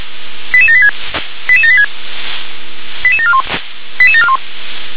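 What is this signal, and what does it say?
Five-tone selective call (ZVEI), heard as quick runs of five tones, tones of about 70 ms. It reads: 88786 88786 88741 88741
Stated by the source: ZVEI five-tone selective call received on an SDR.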